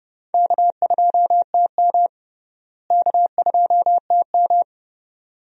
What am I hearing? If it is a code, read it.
Morse code, heard as a keyed tone, about 30 wpm, 700 Hz. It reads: K2TM K2TM
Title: USA Call Signs - 4 Characters - Repeated 2x - 30wpm